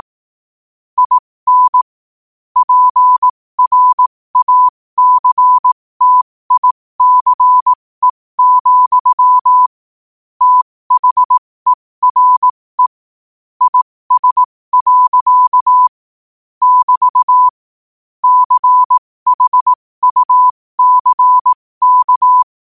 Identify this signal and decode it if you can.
Morse code, heard as a keyed tone, about 18 wpm, spaced as for 15 wpm, 1000 Hz. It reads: IN PRACTICE, THERE IS. <BT> CHUCK
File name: qotdfiles / 15wpm